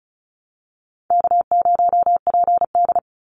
Morse code, heard as a keyed tone, about 34 wpm, 700 Hz. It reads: K0PD
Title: Easy Call Signs Repeated 2x - 35wpm